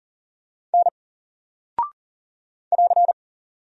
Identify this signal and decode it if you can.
Morse code, heard as a keyed tone, about 39 wpm, 700 Hz. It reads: N <AR>
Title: Single Character (Letter:Number:Punctuation:Prosign) Alt 40wpm